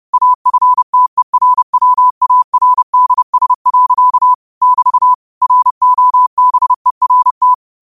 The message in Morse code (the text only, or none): AFTERWARDS. <BT> ROBERT